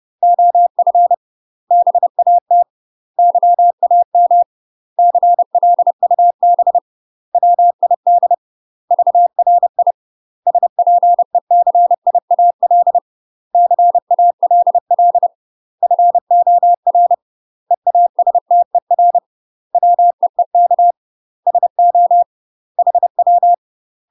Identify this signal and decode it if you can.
Morse code, heard as a keyed tone, about 30 wpm, 700 Hz. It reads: OF BAT YAM CLUB WID VRI SPECIAL CALL FOR EASTER WEEK SO HW
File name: UK MASTER QSOs TRAINING/30WPM